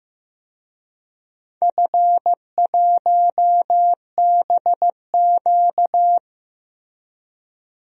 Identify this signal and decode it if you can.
Morse code, heard as a keyed tone, about 15 wpm, 700 Hz. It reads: F1BQ
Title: DXCC-100 Call Signs - 4 Characters - Repeated 2x - 15wpm